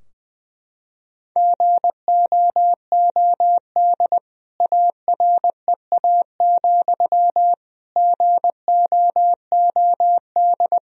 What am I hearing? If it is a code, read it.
Morse code, heard as a keyed tone, about 21 wpm, 700 Hz. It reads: GOOD AREA, GOOD
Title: Sentences from Top 1000 Words - Review Part 2 - 20wpm